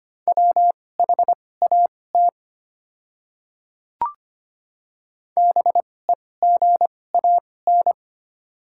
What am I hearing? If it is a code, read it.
Morse code, heard as a keyed tone, about 25 wpm, spaced as for 18 wpm, 700 Hz. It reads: WHAT BEGAN
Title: 2x Character Spacing - Top 300 Words - 25wpm